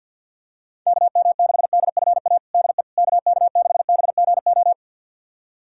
Morse code, heard as a keyed tone, about 49 wpm, 700 Hz. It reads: KM6DFA DE KK6BGO
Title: USA Callsign DE Callsign - 6 Characters - 50wpm